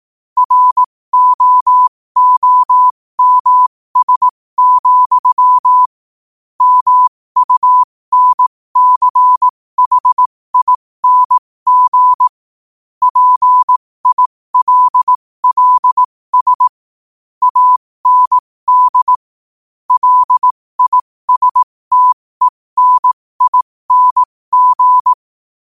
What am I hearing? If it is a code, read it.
Morse code, heard as a keyed tone, about 18 wpm, spaced as for 15 wpm, 1000 Hz. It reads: ROOMS, MUNCHING PILLS AND LISTENING
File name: qotdfiles / 15wpm